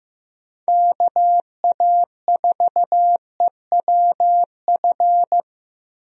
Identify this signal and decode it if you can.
Morse code, heard as a keyed tone, about 15 wpm, 700 Hz. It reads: KA4EWF
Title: USA Call Signs - 6 Characters - Repeated 3x - 15wpm